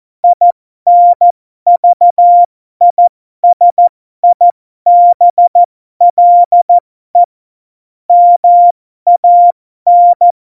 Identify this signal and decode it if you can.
Morse code, heard as a keyed tone, about 14 wpm, spaced as for 12 wpm, 700 Hz. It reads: INVISIBLE MAN
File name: Morsekurs / Witze